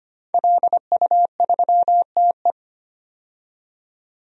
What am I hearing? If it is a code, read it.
Morse code, heard as a keyed tone, about 25 wpm, 700 Hz. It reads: LU3TE
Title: DXCC-100 Call Signs - 5 Characters - 25wpm